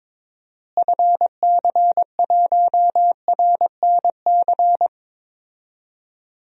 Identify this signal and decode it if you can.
Morse code, heard as a keyed tone, about 22 wpm, 700 Hz. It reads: FC1RNC
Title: DXCC-100 Call Signs - 6 Characters - Repeated 3x - 22wpm